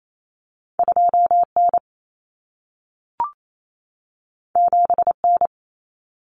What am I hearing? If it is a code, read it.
Morse code, heard as a keyed tone, about 28 wpm, 700 Hz. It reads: 2N 7N
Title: Common Cut Numbers - 2-Digit 28wpm